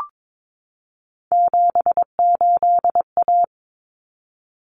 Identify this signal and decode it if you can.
Morse code, heard as a keyed tone, about 22 wpm, 700 Hz. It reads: 78A
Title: Common Cut Numbers - 3-Digit 22wpm